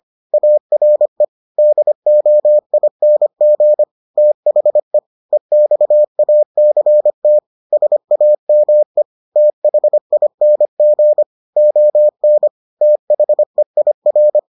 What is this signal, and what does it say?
Morse code, heard as a keyed tone, about 25 wpm, 600 Hz. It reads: ARE DOING THE EXACT SAME THING ON THEIR